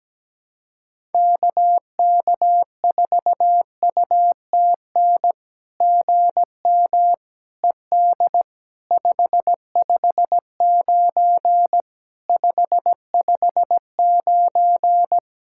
Morse code, heard as a keyed tone, about 17 wpm, 700 Hz. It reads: KK4UTN GM ED 559 559
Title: SOTA QSO Practice 17wpm